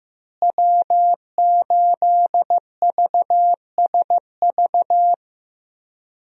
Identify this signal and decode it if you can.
Morse code, heard as a keyed tone, about 15 wpm, 700 Hz. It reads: W8VSV
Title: USA Call Signs - 5 Characters - Repeated 2x - 15wpm